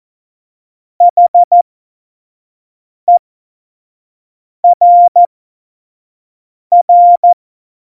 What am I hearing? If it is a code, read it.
Morse code, heard as a keyed tone, about 14 wpm, spaced as for 5 wpm, 700 Hz. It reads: HERR
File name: Morsekurs / Witze / Witz 13. August 2019, 25 F70